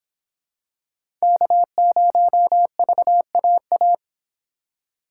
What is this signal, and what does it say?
Morse code, heard as a keyed tone, about 25 wpm, 700 Hz. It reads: K0VAA